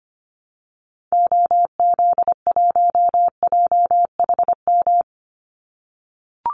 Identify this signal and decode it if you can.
Morse code, heard as a keyed tone, about 25 wpm, 700 Hz. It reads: OZ1JHM
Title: Call Signs - 25wpm